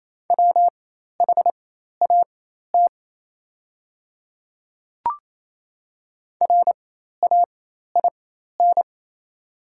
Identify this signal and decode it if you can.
Morse code, heard as a keyed tone, about 27 wpm, spaced as for 13 wpm, 700 Hz. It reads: WHAT RAIN
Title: 4x Character Spacing - Top 300 Words - 28wpm